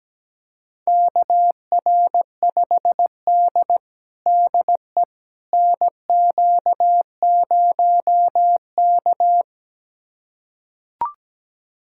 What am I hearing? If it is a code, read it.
Morse code, heard as a keyed tone, about 17 wpm, 700 Hz. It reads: KR5D DE NQ0K
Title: USA Callsign DE Callsign - 4 Characters - 17wpm